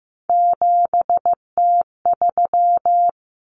Morse code, heard as a keyed tone, about 15 wpm, 700 Hz. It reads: 7T3
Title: Common Cut Numbers - 3-Digit 15wpm